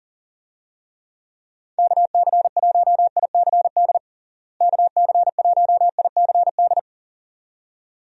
Morse code, heard as a keyed tone, about 40 wpm, 700 Hz. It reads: KC1ICD KC1ICD
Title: USA Call Signs - 6 Characters - Repeated 2x - 40wpm